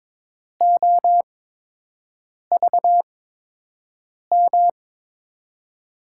Morse code, heard as a keyed tone, about 22 wpm, spaced as for 6 wpm, 700 Hz. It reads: OVM
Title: USA Call Signs - 6 Characters - 8x Character Spacing - 22wpm